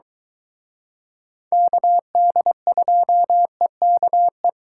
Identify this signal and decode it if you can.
Morse code, heard as a keyed tone, about 24 wpm, 700 Hz. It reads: KD2EKE